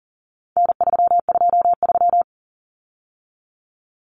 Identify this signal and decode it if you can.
Morse code, heard as a keyed tone, about 40 wpm, 700 Hz. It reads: N323
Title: Common Cut Numbers - 4-Digit 40wpm